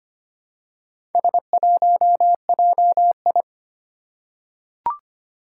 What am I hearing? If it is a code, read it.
Morse code, heard as a keyed tone, about 25 wpm, 700 Hz. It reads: S1JI
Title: Four Challenging Letter-Number Combinations 25wpm